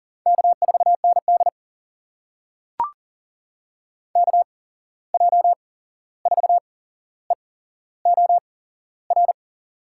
Morse code, heard as a keyed tone, about 42 wpm, spaced as for 15 wpm, 700 Hz. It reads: K4ND KJ4EOR
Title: USA Call Signs - 8x Character Spacing - 40wpm